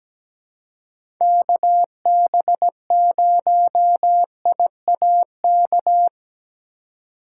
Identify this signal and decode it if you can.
Morse code, heard as a keyed tone, about 17 wpm, 700 Hz. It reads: KB0IAK
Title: USA Call Signs - 6 Characters - 2x Character Spacing - 17wpm